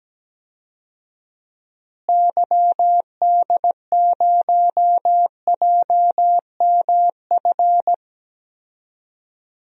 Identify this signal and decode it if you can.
Morse code, heard as a keyed tone, about 17 wpm, 700 Hz. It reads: YD0JMF
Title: DXCC-100 Call Signs - 6 Characters - Repeated 3x - 17wpm